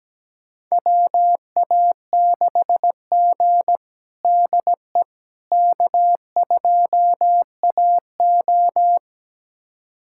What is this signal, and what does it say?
Morse code, heard as a keyed tone, about 17 wpm, 700 Hz. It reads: WA6G DE K2AO
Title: USA Callsign DE Callsign - 4 Characters - 17wpm